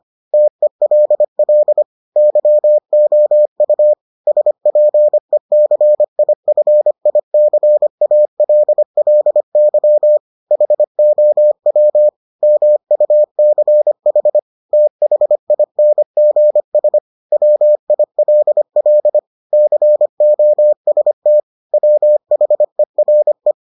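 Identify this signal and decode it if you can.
Morse code, heard as a keyed tone, about 25 wpm, 600 Hz. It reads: TELL YOU SPECIFICALLY HOW MUCH THINGS WILL COST WHERE
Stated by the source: Morse edition of the Foundations of Amateur Radio episodes.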